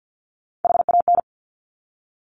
Morse code, heard as a keyed tone, about 50 wpm, 700 Hz. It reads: HAN
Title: Top Three Letter Combinations in Words 50wpm